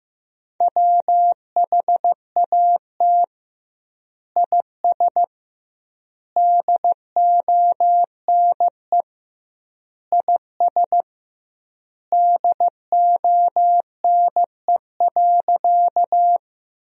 Morse code, heard as a keyed tone, about 16 wpm, 700 Hz. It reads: WHAT IS DONE IS DONE.
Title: 2x Word-Spacing - Sentences from Top 500 Words - 15wpm